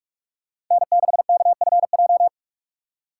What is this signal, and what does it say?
Morse code, heard as a keyed tone, about 42 wpm, 690 Hz. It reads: N6KFJ
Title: USA Call Signs - 5 Characters - 6x Character Spacing - 45wpm